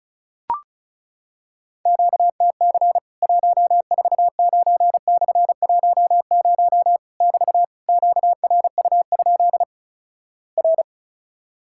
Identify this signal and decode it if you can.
Morse code, heard as a keyed tone, about 36 wpm, 700 Hz. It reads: QTC 149/10 <BT> QRU? R
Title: Worked All Europe DX Contest - QTC Exchange - 35wpm